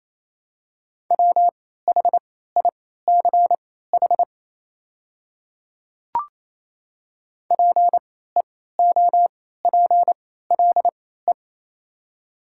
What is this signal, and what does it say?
Morse code, heard as a keyed tone, about 28 wpm, spaced as for 16 wpm, 700 Hz. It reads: WHICH PEOPLE
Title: 3x Character Spacing - Top 200 Words - 28wpm